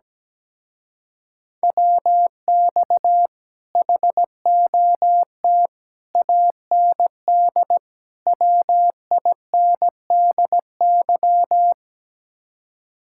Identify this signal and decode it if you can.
Morse code, heard as a keyed tone, about 17 wpm, 700 Hz. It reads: WX HOT AND WINDY
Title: Speed-Racing Common QSO Phrases - 17wpm